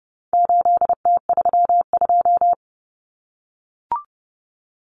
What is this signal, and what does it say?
Morse code, heard as a keyed tone, about 30 wpm, 700 Hz. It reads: 8T32
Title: Common Cut Numbers - 4-Digit 30wpm